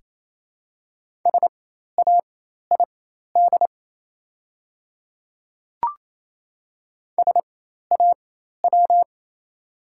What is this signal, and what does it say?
Morse code, heard as a keyed tone, about 27 wpm, spaced as for 13 wpm, 700 Hz. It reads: SAID SAW
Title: 4x Character Spacing - Top 300 Words - 28wpm